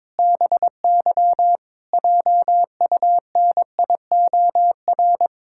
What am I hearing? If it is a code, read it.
Morse code, heard as a keyed tone, about 22 wpm, 700 Hz. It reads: BY JUNIOR
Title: Pangram Challenge v2 - 22wpm